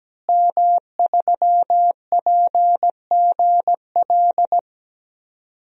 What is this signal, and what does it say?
Morse code, heard as a keyed tone, about 17 wpm, 700 Hz. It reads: M3PGL
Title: DXCC-100 Call Signs - 5 Characters - Repeated 2x - 17wpm